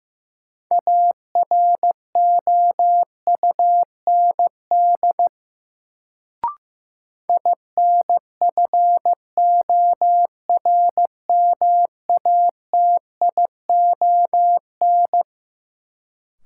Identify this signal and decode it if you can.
Morse code, heard as a keyed tone, about 15 wpm, 700 Hz. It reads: AROUND INFORMATION
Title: Sentences from Top 600 Words - New Word Drill - 15wpm